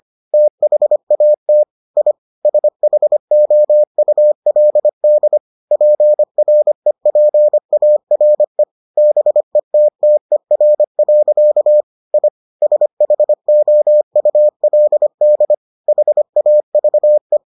Morse code, heard as a keyed tone, about 25 wpm, 600 Hz. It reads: THAT I SHOULD PREPARE BETTER. I SHOULD HAVE